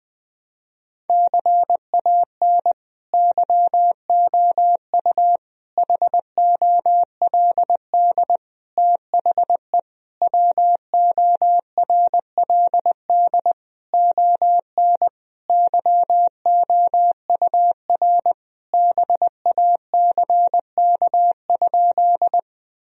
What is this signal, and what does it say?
Morse code, heard as a keyed tone, about 20 wpm, 700 Hz. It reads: CAN YOU HOLD THE WORLD ON YOUR BACK?
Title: Sentences from Top 1000 Words - Review Part 2 - 20wpm